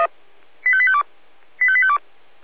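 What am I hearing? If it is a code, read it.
Five-tone selective call (ZVEI), heard as quick runs of five tones, tones of about 70 ms. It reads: 85852 85852
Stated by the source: ZVEI five-tone selective call received on SDR.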